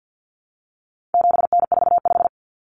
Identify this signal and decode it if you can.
Morse code, heard as a keyed tone, about 48 wpm, 700 Hz. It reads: 7N45
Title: Common Cut Numbers - 4-Digit 50wpm